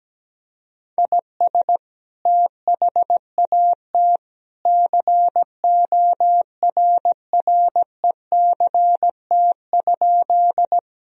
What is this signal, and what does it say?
Morse code, heard as a keyed tone, about 17 wpm, 700 Hz. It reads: IS THAT CORRECT?
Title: Sentences from Top 1000 Words - Review Part 2 - 17wpm